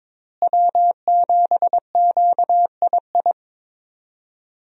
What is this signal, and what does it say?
Morse code, heard as a keyed tone, about 22 wpm, 700 Hz. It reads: W7QII
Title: USA Call Signs - 5 Characters - Repeated 2x - 22wpm